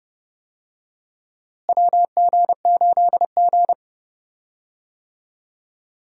Morse code, heard as a keyed tone, about 32 wpm, 700 Hz. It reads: WG8G
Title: USA Call Signs - 4 Characters - Repeated 2x - 30wpm